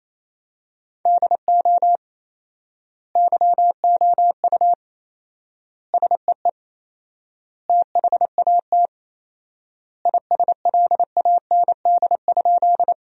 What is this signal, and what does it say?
Morse code, heard as a keyed tone, about 28 wpm, 700 Hz. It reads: DO YOU SEE THAT ISLAND?
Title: 4x Word-Spacing - Sentences from Top 500 Words - 28wpm